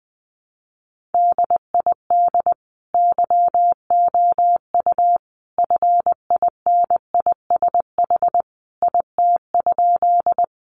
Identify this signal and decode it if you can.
Morse code, heard as a keyed tone, about 20 wpm, 700 Hz. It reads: DID YOU FINISH IT?
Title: Sentences from Top 1200 Words 20wpm